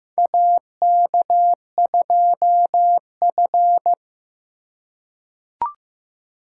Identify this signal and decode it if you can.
Morse code, heard as a keyed tone, about 15 wpm, 700 Hz. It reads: AK2F
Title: USA Call Signs - 4 Characters - 3x Character Spacing - 15wpm